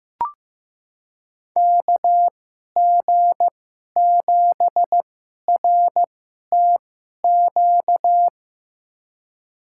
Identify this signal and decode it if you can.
Morse code, heard as a keyed tone, about 15 wpm, spaced as for 11 wpm, 700 Hz. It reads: KG7RTQ
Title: USA Call Signs - 6 Characters - 2x Character Spacing - 15wpm